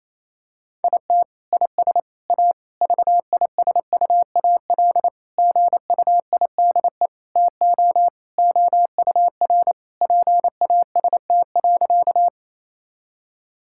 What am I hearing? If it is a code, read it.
Morse code, heard as a keyed tone, about 28 wpm, 700 Hz. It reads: IT IS A VISUAL GUIDE TO OUR PAST.
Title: Sentences from Top 1000 Words - Review Part 6 - 28wpm